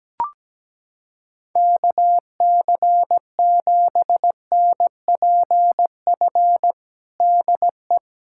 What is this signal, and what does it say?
Morse code, heard as a keyed tone, about 17 wpm, 700 Hz. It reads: KC7NPF DE
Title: USA Callsign DE Callsign - 6 Characters - 17wpm